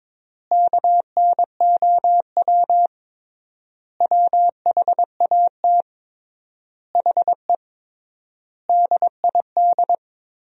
Morse code, heard as a keyed tone, about 21 wpm, 700 Hz. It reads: KNOW WHAT HE DID
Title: 3x Word-Spacing - Sentences from Top 300 Words - 22wpm